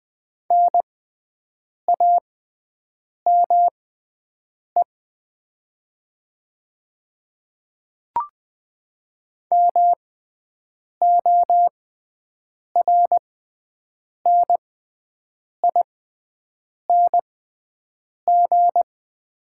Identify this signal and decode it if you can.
Morse code, heard as a keyed tone, about 20 wpm, spaced as for 7 wpm, 700 Hz. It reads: NAME MORNING
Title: 6x Character Spacing - Top 500 Words - 20wpm